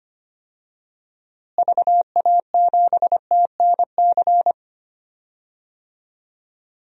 Morse code, heard as a keyed tone, about 25 wpm, 700 Hz. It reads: VA7TNC
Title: DXCC-100 Call Signs - 6 Characters - Repeated 3x - 25wpm